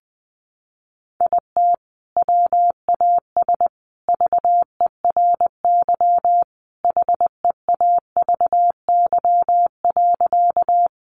Morse code, heard as a keyed tone, about 20 wpm, 700 Hz. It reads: IT WAS VERY HEAVY.
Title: Sentences from Top 1100 Words 20wpm